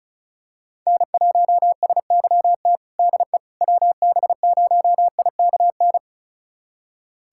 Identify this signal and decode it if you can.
Morse code, heard as a keyed tone, about 35 wpm, 700 Hz. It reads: N1SYT DE WB0IKN
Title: USA Callsign DE Callsign 35wpm